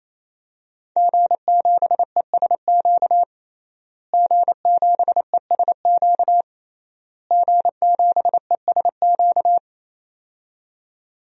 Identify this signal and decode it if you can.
Morse code, heard as a keyed tone, about 28 wpm, 700 Hz. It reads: G7ESQ G7ESQ G7ESQ
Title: DXCC-100 Call Signs - 5 Characters - Repeated 3x - 28wpm